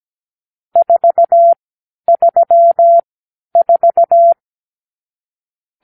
Morse code, heard as a keyed tone, about 17 wpm, spaced as for 10 wpm, 680 Hz. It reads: 434
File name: k7qo / Lesson 52 (8)